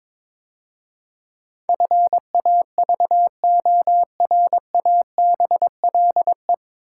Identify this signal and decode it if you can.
Morse code, heard as a keyed tone, about 22 wpm, 700 Hz. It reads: FAVORABLE
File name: 9 Letter Words - First 3 Letters Given - 22wpm